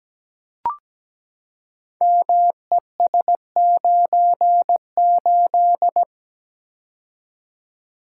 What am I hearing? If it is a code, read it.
Morse code, heard as a keyed tone, about 17 wpm, 700 Hz. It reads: MES98
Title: Code Groups - 5 Letter:Numbers - 17wpm